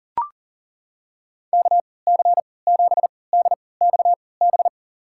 Morse code, heard as a keyed tone, about 40 wpm, spaced as for 23 wpm, 700 Hz. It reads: KC7DXB
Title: USA Call Signs - 3x Character Spacing - 40wpm